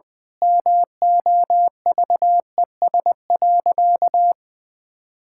Morse code, heard as a keyed tone, about 21 wpm, 700 Hz. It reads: MOVES.
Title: Pangram Challenge v2 - 20wpm